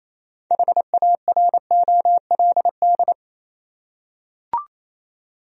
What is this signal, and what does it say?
Morse code, heard as a keyed tone, about 28 wpm, 700 Hz. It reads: HAROLD
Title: USA Common Names - 28wpm